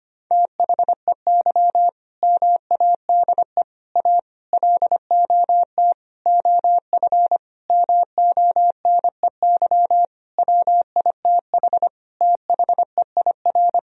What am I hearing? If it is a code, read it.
Morse code, heard as a keyed tone, about 25 wpm, 700 Hz. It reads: THEY MADE A LOT OF MONEY WITH THEIR
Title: Sentences from Top 1000 Words - Review Part 4 - 25wpm